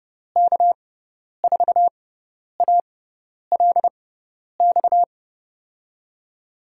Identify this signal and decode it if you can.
Morse code, heard as a keyed tone, about 30 wpm, spaced as for 10 wpm, 700 Hz. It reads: K4ALX
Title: USA Call Signs - 6 Characters - 6x Character Spacing - 30wpm